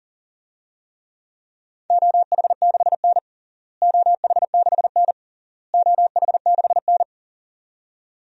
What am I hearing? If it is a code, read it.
Morse code, heard as a keyed tone, about 40 wpm, 700 Hz. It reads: OH6N OH6N OH6N
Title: DXCC-100 Call Signs - 4 Characters - Repeated 3x - 40wpm